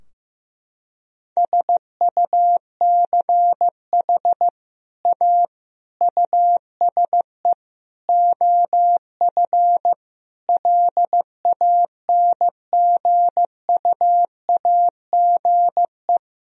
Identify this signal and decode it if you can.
Morse code, heard as a keyed tone, about 15 wpm, 700 Hz. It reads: SUCH A USE OF LANGUAGE
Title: Sentences from Top 1000 Words - Review Part 3 - 15wpm